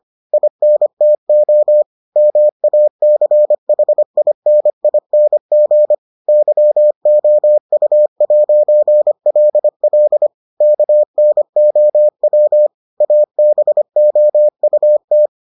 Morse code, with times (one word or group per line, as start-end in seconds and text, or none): 0.3-1.8 INTO
2.2-5.9 MACHINING
6.3-10.3 YOU'LL
10.6-12.7 KNOW
13.0-15.3 ABOUT